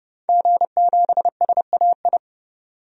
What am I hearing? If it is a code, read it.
Morse code, heard as a keyed tone, about 30 wpm, 700 Hz. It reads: G7SAI
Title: DXCC-100 Call Signs - 5 Characters - 30wpm